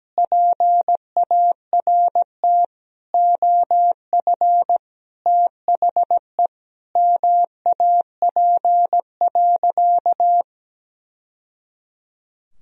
Morse code, heard as a keyed tone, about 17 wpm, 700 Hz. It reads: PART OF THE MAP.
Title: Sentences from Top 1000 Words - Review Part 2 - 17wpm